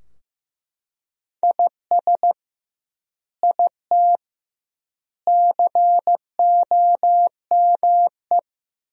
Morse code, heard as a keyed tone, about 15 wpm, 700 Hz. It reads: IS IT COME
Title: 2x Word-Spacing - Sentences from Top 500 Words - 15wpm